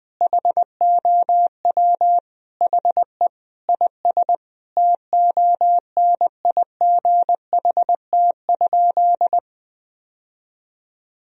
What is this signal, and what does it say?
Morse code, heard as a keyed tone, about 20 wpm, 700 Hz. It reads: HOW HE IS TONIGHT?
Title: Sentences from Top 1000 Words - Review Part 3 - 20wpm